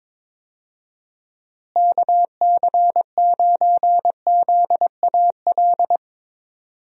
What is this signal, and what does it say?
Morse code, heard as a keyed tone, about 22 wpm, 700 Hz. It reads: KC9ZAL
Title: USA Call Signs - 6 Characters - Repeated 2x - 22wpm